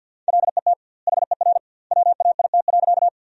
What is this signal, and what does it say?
Morse code, heard as a keyed tone, about 50 wpm, 700 Hz. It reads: LET HER WAIT.
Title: 2x Word-Spacing - Sentences from Top 500 Words - 50wpm